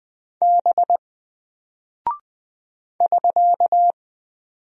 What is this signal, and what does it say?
Morse code, heard as a keyed tone, about 20 wpm, 700 Hz. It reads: B <SK>
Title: Single Character (Letter:Number:Punctuation:Prosign) Alt 20wpm